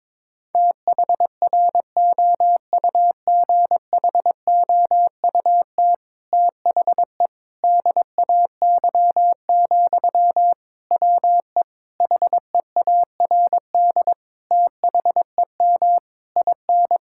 Morse code, heard as a keyed tone, about 22 wpm, 700 Hz. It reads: THROUGHOUT THE DAY, WE HEARD THEM IN
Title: Sentences from Top 1000 Words - Review Part 5 - 22wpm